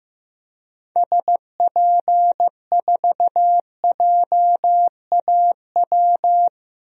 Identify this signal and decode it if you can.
Morse code, heard as a keyed tone, about 15 wpm, 700 Hz. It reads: SP4JAW
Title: DXCC-100 Call Signs - 6 Characters - 15wpm